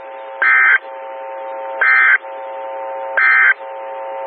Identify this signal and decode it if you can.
Emergency Alert System (SAME) tones, heard as harsh buzzing bursts. It is heard from 0.3 to 3.6 s.
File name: WX9MCS 147255 MHz SKYWARN Repeater